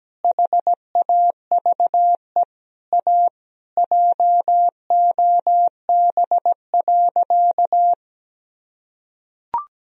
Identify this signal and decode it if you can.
Morse code, heard as a keyed tone, about 17 wpm, 700 Hz. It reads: HAVE A JOB.
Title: Sentences from Top 1000 Words - Review Part 3 - 17wpm